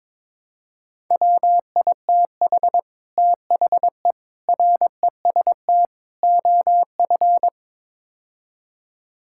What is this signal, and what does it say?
Morse code, heard as a keyed tone, about 22 wpm, 700 Hz. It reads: WITH THE REST OF
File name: practice / 22wpm / Build A Phrase - 5-Word Phrases - 22wpm